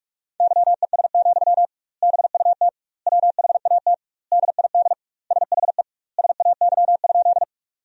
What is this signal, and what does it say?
Morse code, heard as a keyed tone, about 46 wpm, 700 Hz. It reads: YES, BUT WHAT DID SHE SAY?
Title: 2x Word-Spacing - Sentences from Top 500 Words - 45wpm